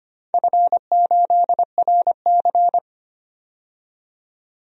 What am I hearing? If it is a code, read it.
Morse code, heard as a keyed tone, about 25 wpm, 700 Hz. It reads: F8RC
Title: DXCC-100 Call Signs - 4 Characters - 25wpm